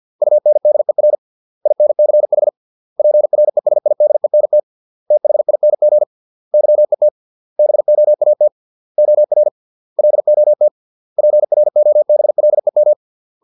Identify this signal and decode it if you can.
Morse code, heard as a keyed tone, about 50 wpm, 600 Hz. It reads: UNDER INCH PRESIDENT THING YET BOAT OR LOT PROBLEM